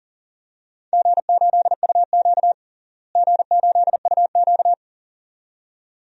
Morse code, heard as a keyed tone, about 40 wpm, 700 Hz. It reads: G8UQ G8UQ
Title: DXCC-100 Call Signs - 4 Characters - Repeated 2x - 40wpm